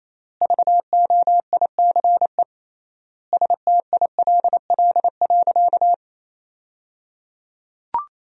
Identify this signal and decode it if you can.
Morse code, heard as a keyed tone, about 28 wpm, 700 Hz. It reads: VOICE STILL.
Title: 3x Word-Spacing - Sentences from Top 500 Words - 28wpm